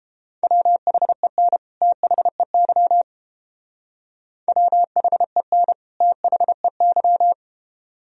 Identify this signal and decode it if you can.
Morse code, heard as a keyed tone, about 30 wpm, 700 Hz. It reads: WHEN THEY WHEN THEY